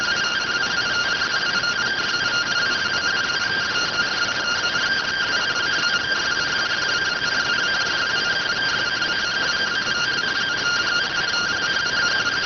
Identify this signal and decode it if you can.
Two-tone teleprinter signal, about 170 Hz shift.